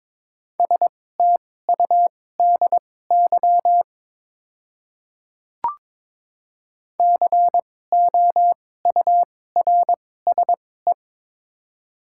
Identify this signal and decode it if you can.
Morse code, heard as a keyed tone, about 22 wpm, spaced as for 16 wpm, 700 Hz. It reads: STUDY COURSE
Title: 2x Character Spacing - Top 500 Words - 22wpm